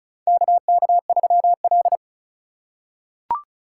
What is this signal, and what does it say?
Morse code, heard as a keyed tone, about 36 wpm, 700 Hz. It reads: KK3L
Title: USA Call Signs - 4 Characters - 2x Character Spacing - 35wpm